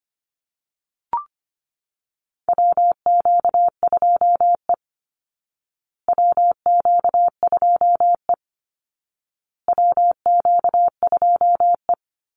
Morse code, heard as a keyed tone, about 25 wpm, 700 Hz. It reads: WQ2E WQ2E WQ2E
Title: Call Signs Repeated 2x - 25wpm